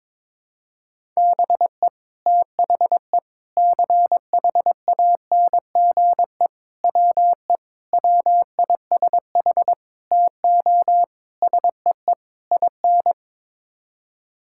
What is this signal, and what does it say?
Morse code, heard as a keyed tone, about 22 wpm, 700 Hz. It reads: BE THE CHANGE WE WISH TO SEE IN
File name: practice / 22wpm / Sets of 8 Words - 22wpm